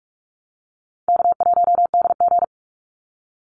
Morse code, heard as a keyed tone, about 44 wpm, 700 Hz. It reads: K1DG
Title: Easy Call Signs Repeated 3x - 45wpm